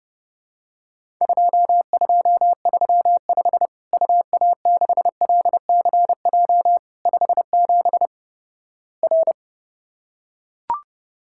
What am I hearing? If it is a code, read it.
Morse code, heard as a keyed tone, about 32 wpm, 700 Hz. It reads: 2235 UA6LCJ 57 R
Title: Worked All Europe DX Contest - QTC Exchange - 30wpm